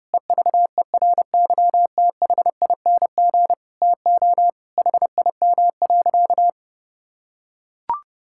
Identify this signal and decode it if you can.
Morse code, heard as a keyed tone, about 32 wpm, 700 Hz. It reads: EVERYTHING TO HIM.
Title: Sentences from Top 1000 Words - Review Part 6 - 30wpm